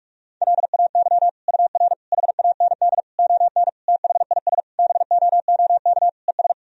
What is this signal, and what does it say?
Morse code, heard as a keyed tone, about 46 wpm, 700 Hz. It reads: LAY UR HAND ON THIS BOOK ES
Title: Sentences from Top 400 Words and Abbreviations 45wpm